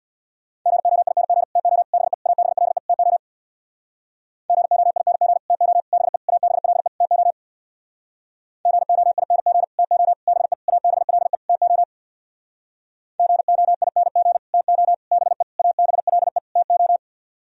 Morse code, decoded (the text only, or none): GOING TO BE ABLE TO GOING TO BE ABLE TO GOING TO BE ABLE TO GOING TO BE ABLE TO